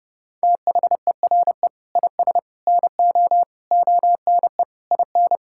Morse code, heard as a keyed tone, about 30 wpm, 700 Hz. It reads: THERE IS NO ONE IN
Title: Sentences from Top 1000 Words - Review Part 6 - 30wpm